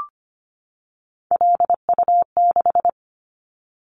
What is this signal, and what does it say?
Morse code, heard as a keyed tone, about 25 wpm, 690 Hz. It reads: LU6
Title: Callsign Prefixes for Top 50 DXCC - 25wpm